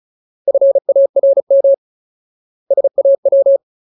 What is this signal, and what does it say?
Morse code, heard as a keyed tone, about 34 wpm, 550 Hz. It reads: FARM SAW
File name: Ultimate Word List / 35WPM Ultimate